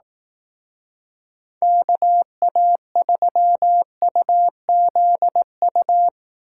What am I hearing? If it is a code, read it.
Morse code, heard as a keyed tone, about 18 wpm, 700 Hz. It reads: KA3UZU
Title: Speed Racing - USA 2x3 Call Signs 15wpm